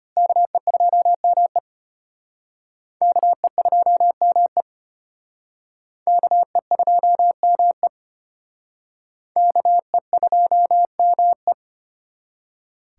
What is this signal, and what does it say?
Morse code, heard as a keyed tone, about 33 wpm, 700 Hz. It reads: KE2ME KE2ME KE2ME KE2ME